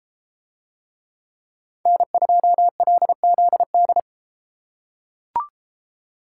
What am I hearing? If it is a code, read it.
Morse code, heard as a keyed tone, about 30 wpm, 700 Hz. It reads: N2LZD